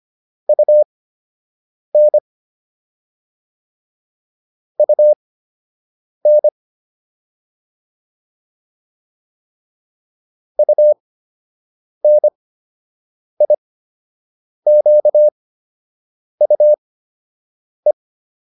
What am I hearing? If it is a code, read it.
Morse code, heard as a keyed tone, about 25 wpm, spaced as for 7 wpm, 600 Hz. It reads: UN UN UNIQUE